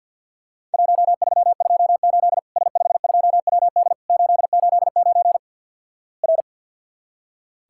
Morse code, heard as a keyed tone, about 46 wpm, 700 Hz. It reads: 1328 S52WD 889 R
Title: Worked All Europe DX Contest - QTC Exchange - 50wpm